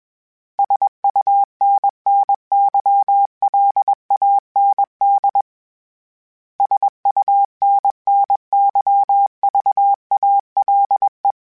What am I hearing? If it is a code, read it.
Morse code, heard as a keyed tone, about 21 wpm, 800 Hz. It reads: SUNNYLAND SUNNYVALE
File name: fromSMs / FromSMs mp3s / Cities@20